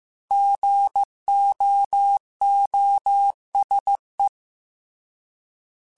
Morse code, heard as a keyed tone, about 15 wpm, 800 Hz. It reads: GOOSE
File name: Alphabetical / Alphabetical mp3s / Gs @13